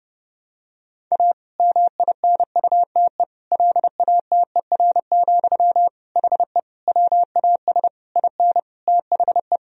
Morse code, heard as a keyed tone, about 29 wpm, 700 Hz. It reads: A MINUTE LATER, HE WAS IN THE
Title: Sentences from Top 1000 Words - Review Part 4 - 30wpm